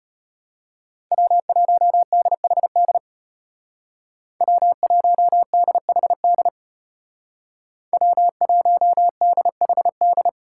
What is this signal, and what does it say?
Morse code, heard as a keyed tone, about 30 wpm, 700 Hz. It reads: W1DHD W1DHD W1DHD
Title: Speed Racing - USA 1x3 Call Signs 25wpm